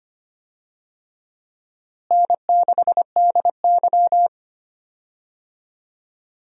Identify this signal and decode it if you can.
Morse code, heard as a keyed tone, about 25 wpm, 700 Hz. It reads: N6DY
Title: USA Call Signs - 4 Characters - Repeated 3x - 25wpm